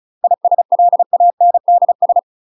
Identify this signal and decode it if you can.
Morse code, heard as a keyed tone, about 35 wpm, 700 Hz. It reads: ISLANDS
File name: UK MASTER QSOs TRAINING / 35WPM / UK MASTER QSOs 0021